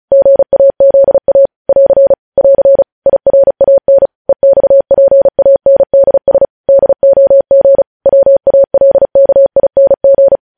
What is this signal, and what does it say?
Morse code, heard as a keyed tone, about 36 wpm, 550 Hz. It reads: GAZA <AR> <AR> IRAN EXPANDS DOG WALKING
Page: news headlines in morse code at 35 wpm